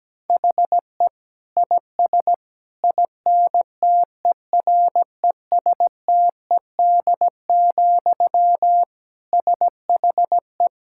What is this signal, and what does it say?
Morse code, heard as a keyed tone, about 17 wpm, 700 Hz. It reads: HE IS INTERESTED, SHE
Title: Sentences from Top 1000 Words - Review Part 6 - 17wpm